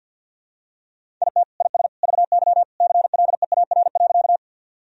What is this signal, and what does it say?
Morse code, heard as a keyed tone, about 52 wpm, 700 Hz. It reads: IT IS VY CLEAR.
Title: Sentences from Top 500 Words and Abbreviations 50wpm